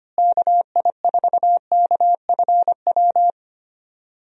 Morse code, heard as a keyed tone, about 25 wpm, 700 Hz. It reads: KI4KFW
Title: USA Call Signs - 6 Characters - Repeated 3x - 25wpm